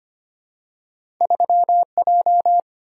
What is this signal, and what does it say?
Morse code, heard as a keyed tone, about 25 wpm, 700 Hz. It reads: 3J
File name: Two Letter-Number Combinations - 25wpm